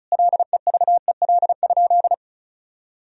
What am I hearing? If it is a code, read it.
Morse code, heard as a keyed tone, about 34 wpm, 700 Hz. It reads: LEVEL?
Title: Sentences from Top 1000 Words - Review Part 4 - 35wpm